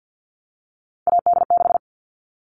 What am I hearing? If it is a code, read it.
Morse code, heard as a keyed tone, about 50 wpm, 700 Hz. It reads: AD6
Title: All Cut Numbers - 3-Digit 50wpm